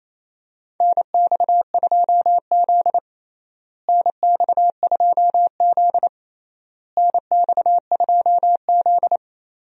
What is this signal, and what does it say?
Morse code, heard as a keyed tone, about 27 wpm, 700 Hz. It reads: NX2Z NX2Z NX2Z
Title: USA Call Signs - 4 Characters - Repeated 3x - 28wpm